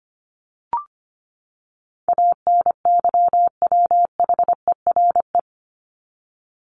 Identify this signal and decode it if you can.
Morse code, heard as a keyed tone, about 25 wpm, 700 Hz. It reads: ANYWHERE
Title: Top 1251-1500 Words 25wpm